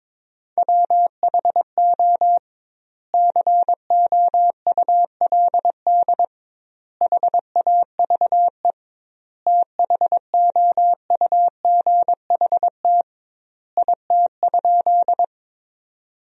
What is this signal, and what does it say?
Morse code, heard as a keyed tone, about 22 wpm, 700 Hz. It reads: WHO COULD HAVE THOUGHT IT?
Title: 2x Word-Spacing - Sentences from Top 300 Words - 22wpm